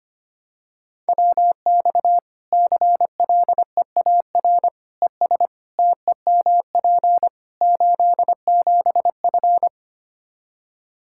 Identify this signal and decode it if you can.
Morse code, heard as a keyed tone, about 25 wpm, 700 Hz. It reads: WX CLEAR ES TEMP 87F
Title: Practice QSO - Exchange 2 - Info - 25wpm